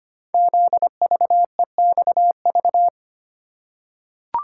Morse code, heard as a keyed tone, about 25 wpm, 700 Hz. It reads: ZVEXV
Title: Code Groups - 5 Letters - 25wpm